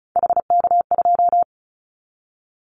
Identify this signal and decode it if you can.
Morse code, heard as a keyed tone, about 34 wpm, 700 Hz. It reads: HK2
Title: Callsign Prefixes for Top 100 DXCC - 35wpm